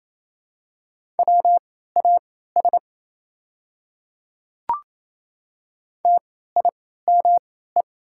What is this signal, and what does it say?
Morse code, heard as a keyed tone, about 27 wpm, spaced as for 16 wpm, 690 Hz. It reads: WAS TIME
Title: 3x Character Spacing - Top 100 Words - 28wpm